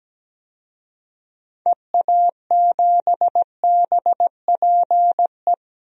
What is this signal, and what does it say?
Morse code, heard as a keyed tone, about 17 wpm, 700 Hz. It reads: EA7BPE
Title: DXCC-100 Call Signs - 6 Characters - 17wpm